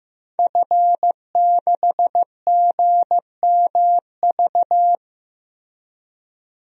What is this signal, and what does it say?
Morse code, heard as a keyed tone, about 15 wpm, 700 Hz. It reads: F6GMV
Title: DXCC-100 Call Signs - 5 Characters - Repeated 3x - 15wpm